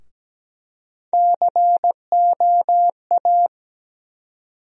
Morse code, heard as a keyed tone, about 17 wpm, 700 Hz. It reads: COA
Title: USA Call Sign Postfixes - 3 Letters 17wpm